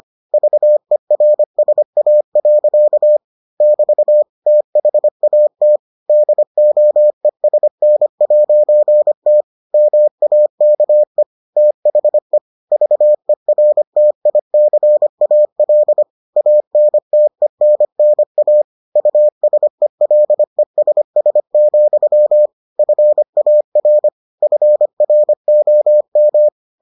Morse code, heard as a keyed tone, about 25 wpm, 600 Hz. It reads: VERSA. <BT> THAT DOESN'T MAKE THE VERTICAL ANTENNA USELESS, FAR FROM